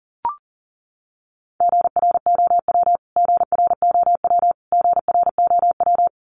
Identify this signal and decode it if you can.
Morse code, heard as a keyed tone, about 40 wpm, 700 Hz. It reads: GROW GROW GROW
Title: Top 300 Words Repeated 40wpm